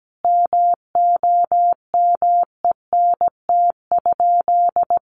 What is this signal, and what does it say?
Morse code, heard as a keyed tone, about 17 wpm, 700 Hz. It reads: MOMENT?
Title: Sentences from Top 1000 Words 17wpm